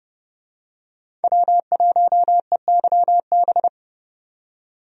Morse code, heard as a keyed tone, about 32 wpm, 700 Hz. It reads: W1EYB
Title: USA Call Signs - 5 Characters - 30wpm